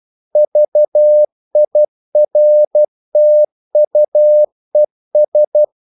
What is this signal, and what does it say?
Morse code, heard as a keyed tone, about 12 wpm, 600 Hz. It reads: VIRTUES
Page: English quotations in CW (telegraphy - morse)